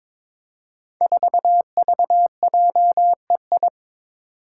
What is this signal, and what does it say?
Morse code, heard as a keyed tone, about 22 wpm, 700 Hz. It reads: 4VJEI
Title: Five Challenging Letter-Number Combinations 22wpm